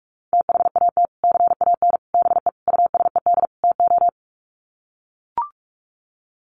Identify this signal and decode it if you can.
Morse code, heard as a keyed tone, about 44 wpm, 700 Hz. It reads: THAT CAN BE USED TO
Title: Sets of 5 Words Encore 45wpm